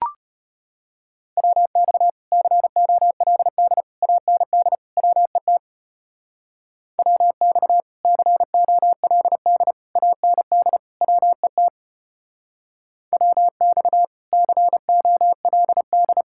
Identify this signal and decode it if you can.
Morse code, heard as a keyed tone, about 32 wpm, 700 Hz. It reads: WX COLD AND WET WX COLD AND WET WX COLD